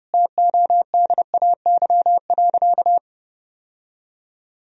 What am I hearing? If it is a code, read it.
Morse code, heard as a keyed tone, about 30 wpm, 700 Hz. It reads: TODAY.
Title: Sentences from Top 1000 Words - Review Part 4 - 30wpm